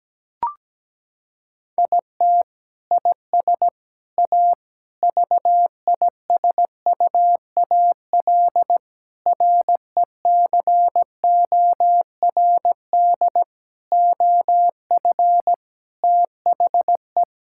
Morse code, heard as a keyed tone, about 17 wpm, 700 Hz. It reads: IT IS A VISUAL RECORD OF THE
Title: Sentences from Top 1000 Words - Review Part 6 - 17wpm